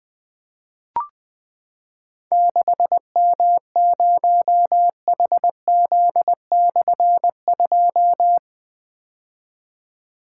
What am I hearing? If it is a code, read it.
Morse code, heard as a keyed tone, about 20 wpm, 700 Hz. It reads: 6M0HZ/2
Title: Difficult Call Signs 20wpm